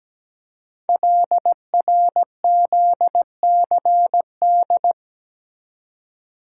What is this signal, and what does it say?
Morse code, heard as a keyed tone, about 17 wpm, 700 Hz. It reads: LRZCD
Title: Code Groups - 5 Letters - 17wpm